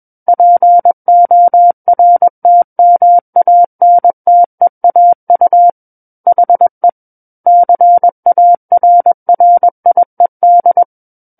Morse code, heard as a keyed tone, about 21 wpm, 700 Hz. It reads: PORTMANTEAU HE CARRIED
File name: Book One / 12-Chapter 12 - 21 WPM